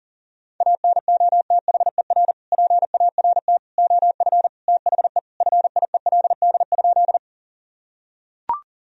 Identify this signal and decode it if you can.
Morse code, heard as a keyed tone, about 40 wpm, 700 Hz. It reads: ANOTHER PART OF THE FIELD?
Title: Sentences from Top 1000 Words - Review Part 2 - 40wpm